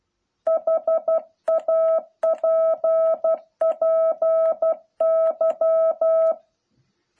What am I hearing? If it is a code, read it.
Morse code, heard as a keyed tone, about 12 wpm, 660 Hz. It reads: HAPPY